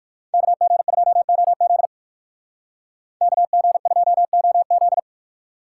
Morse code, heard as a keyed tone, about 59 wpm, 700 Hz. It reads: KG2OZ KG2OZ